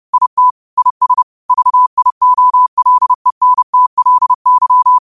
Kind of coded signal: Morse code